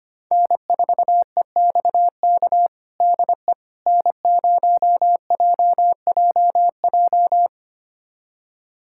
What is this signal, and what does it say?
Morse code, heard as a keyed tone, about 25 wpm, 700 Hz. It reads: N4EXK DE N0JJJ
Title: USA Callsign DE Callsign - 5 Characters - 25wpm